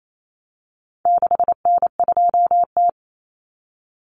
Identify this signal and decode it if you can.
Morse code, heard as a keyed tone, about 28 wpm, 700 Hz. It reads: 6N2T
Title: Common Cut Numbers - 4-Digit 28wpm